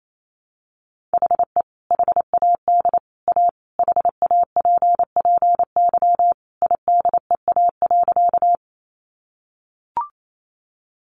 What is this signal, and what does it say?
Morse code, heard as a keyed tone, about 28 wpm, 700 Hz. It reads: HE HAD A HAPPY IDEA.
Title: Sentences from Top 1000 Words 28wpm